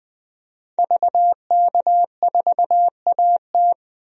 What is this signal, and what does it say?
Morse code, heard as a keyed tone, about 21 wpm, 700 Hz. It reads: VK4AT
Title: DXCC-100 Call Signs - 5 Characters - 20wpm